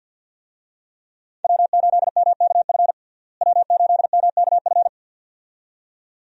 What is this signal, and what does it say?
Morse code, heard as a keyed tone, about 52 wpm, 700 Hz. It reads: W8MKF W8MKF
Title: USA Call Signs - 5 Characters - Repeated 2x - 50wpm